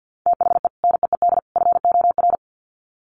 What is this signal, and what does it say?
Morse code, heard as a keyed tone, about 50 wpm, 700 Hz. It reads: THE NEED FOR
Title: Sets of 3 Words 50wpm